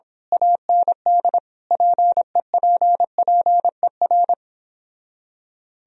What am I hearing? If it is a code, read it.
Morse code, heard as a keyed tone, about 26 wpm, 700 Hz. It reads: AND PEPPER
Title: Speed-Racing - Sets of 5 Words 22wpm